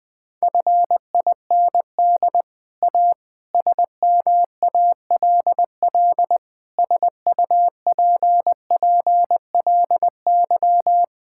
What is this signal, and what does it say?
Morse code, heard as a keyed tone, about 20 wpm, 700 Hz. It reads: FIND A SMALL SUPPLY
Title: Sentences from Top 1000 Words - Review Part 5 - 20wpm